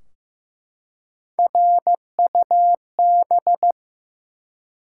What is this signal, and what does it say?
Morse code, heard as a keyed tone, about 15 wpm, 700 Hz. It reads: RUB
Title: Three Letter Words - 15wpm